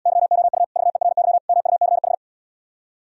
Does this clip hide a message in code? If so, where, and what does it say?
Yes: Morse code from 0.1 to 2.1 s, heard as a keyed tone, about 77 wpm, 700 Hz. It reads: YOU VERY MUCH